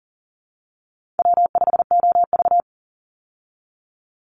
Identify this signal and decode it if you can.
Morse code, heard as a keyed tone, about 40 wpm, 700 Hz. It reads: W5OV
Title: Easy Call Signs Repeated 3x - 40wpm